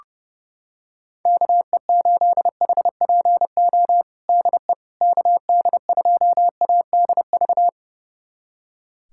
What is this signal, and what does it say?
Morse code, heard as a keyed tone, about 30 wpm, 700 Hz. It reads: KE8HPO DE KD2ADV
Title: USA Callsign DE Callsign - 6 Characters - 30wpm